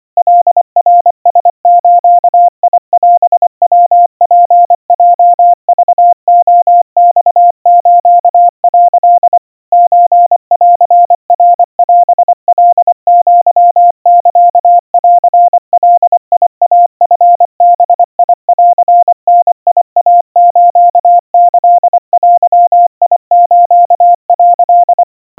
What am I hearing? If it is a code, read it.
Morse code (Japanese Wabun code), heard as a keyed tone, about 24 wpm, 700 Hz. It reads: カナラズオヤツヲクレマス」 ソンナオカアサンガイチバンダイスキデス」
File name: 90 cpm 700 Hz Crispy